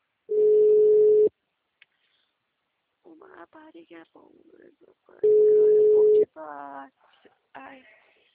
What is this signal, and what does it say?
Ringback tone (Europe, 425 Hz), with speech.
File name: out- -2016- - - .74056